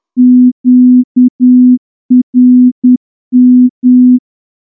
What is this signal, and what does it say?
Morse code, heard as a keyed tone, about 10 wpm, 250 Hz. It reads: QRM
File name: morse-training-10WPM